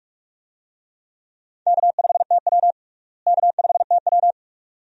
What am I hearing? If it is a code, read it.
Morse code, heard as a keyed tone, about 41 wpm, 700 Hz. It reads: K5TW K5TW
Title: USA Call Signs - 4 Characters - Repeated 3x - 45wpm